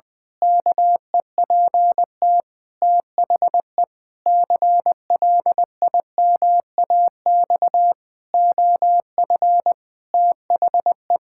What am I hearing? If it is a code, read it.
Morse code, heard as a keyed tone, about 21 wpm, 700 Hz. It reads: KEPT THE CLIMAX OF THE
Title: Pangram Challenge v2 - 20wpm